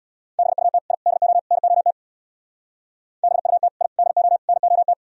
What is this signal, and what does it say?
Morse code, heard as a keyed tone, about 64 wpm, 700 Hz. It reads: BUT I DO NOT BUT I DO NOT